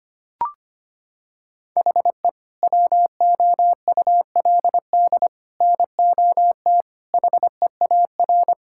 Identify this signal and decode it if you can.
Morse code, heard as a keyed tone, about 25 wpm, 700 Hz. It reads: HE WOULD NOT HEAR
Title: Sentences from Top 1000 Words - Review Part 5 - 25wpm